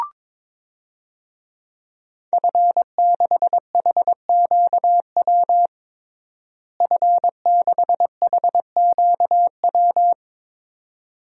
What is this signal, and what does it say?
Morse code, heard as a keyed tone, about 22 wpm, 700 Hz. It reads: F6HQW F6HQW
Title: DXCC-100 Call Signs - 5 Characters - Repeated 2x - 22wpm